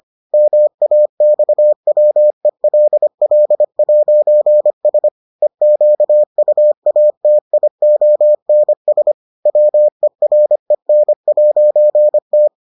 Morse code, heard as a keyed tone, about 25 wpm, 600 Hz. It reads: MAXWELL'S EQUATIONS WEREN'T